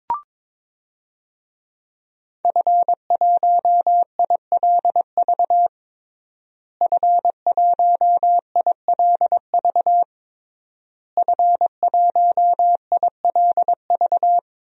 Morse code, heard as a keyed tone, about 22 wpm, 700 Hz. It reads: F1ILV F1ILV F1ILV
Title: DXCC-100 Call Signs - 5 Characters - Repeated 3x - 22wpm